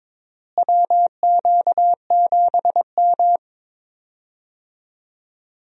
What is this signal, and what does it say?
Morse code, heard as a keyed tone, about 22 wpm, 700 Hz. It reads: WQ7M
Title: USA Call Signs - 4 Characters - Repeated 3x - 22wpm